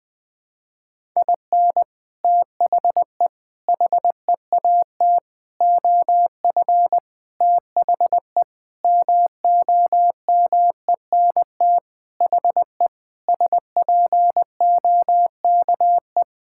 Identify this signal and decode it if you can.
Morse code, heard as a keyed tone, about 21 wpm, 700 Hz. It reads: IN THE HEAT OF THE MOMENT HE SPOKE
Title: Sentences with 6-Word Phrases - Spoken Beginning - 20wpm